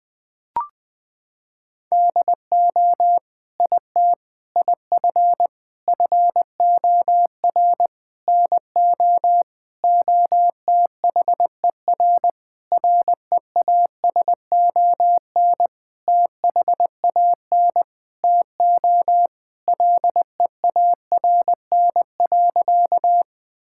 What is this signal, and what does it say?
Morse code, heard as a keyed tone, about 20 wpm, 700 Hz. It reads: DO IT IF FOR NO OTHER REASON THAN TO LEARN.
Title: Sentences with 7-Word Phrases - 20wpm